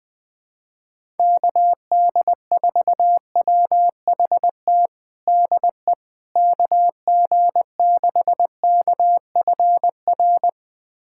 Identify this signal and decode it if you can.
Morse code, heard as a keyed tone, about 21 wpm, 700 Hz. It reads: KD4WHT DE KG6KFR
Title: USA Callsign DE Callsign - 6 Characters - 20wpm